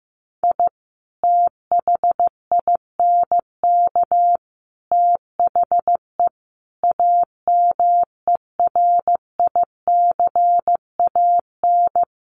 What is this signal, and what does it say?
Morse code, heard as a keyed tone, about 15 wpm, 700 Hz. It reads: I THINK THE AMERICAN
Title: Sets of 5 Words Encore 15wpm